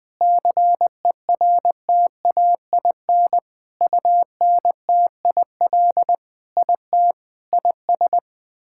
Morse code, heard as a keyed tone, about 20 wpm, 700 Hz. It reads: CERTAIN UNTIL IT IS
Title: Sentences with 8-Word Phrases - 20wpm